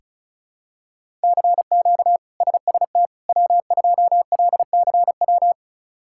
Morse code, heard as a keyed tone, about 36 wpm, 700 Hz. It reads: CQ SST W2LCW
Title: K1USN SST QSO Practice - 35wpm